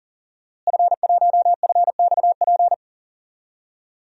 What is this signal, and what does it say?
Morse code, heard as a keyed tone, about 40 wpm, 700 Hz. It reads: F1FXP
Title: DXCC-100 Call Signs - 5 Characters - Repeated 3x - 40wpm